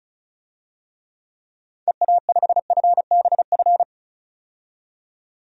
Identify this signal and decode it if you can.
Morse code, heard as a keyed tone, about 34 wpm, 700 Hz. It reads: EA5FBF
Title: DXCC-100 Call Signs - 6 Characters - 35wpm